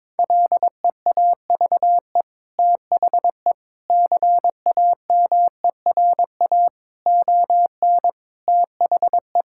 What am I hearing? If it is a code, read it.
Morse code, heard as a keyed tone, about 21 wpm, 700 Hz. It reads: LEAVE THE CAMERA ON THE
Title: Sentences from Top 1000 Words - Review Part 5 - 22wpm